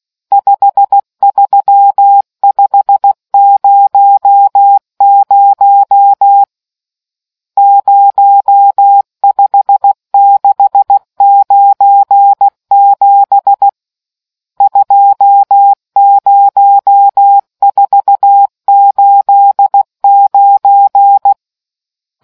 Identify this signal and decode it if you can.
Morse code, heard as a keyed tone, about 16 wpm, 800 Hz. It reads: 53500 05697 20489